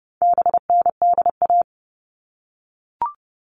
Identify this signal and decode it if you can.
Morse code, heard as a keyed tone, about 30 wpm, 700 Hz. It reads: BNDA
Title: All Cut Numbers - 4-Digit 30wpm